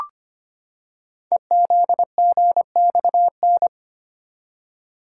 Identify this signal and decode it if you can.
Morse code, heard as a keyed tone, about 25 wpm, 700 Hz. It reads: EZGXN
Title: Code Groups - 5 Letter:Numbers - 25wpm